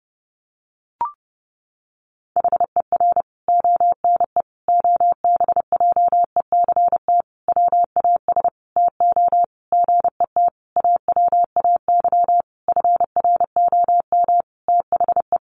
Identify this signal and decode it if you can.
Morse code, heard as a keyed tone, about 30 wpm, 700 Hz. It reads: HER ONE OBJECT WAS TO GET AWAY FROM THE
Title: Sentences from Top 1500 Words 30wpm